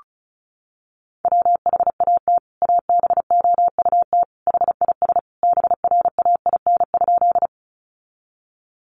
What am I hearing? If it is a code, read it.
Morse code, heard as a keyed tone, about 34 wpm, 700 Hz. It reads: WHAT ABOUT HIS BRAIN?
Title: Sentences from Top 1100 Words 35wpm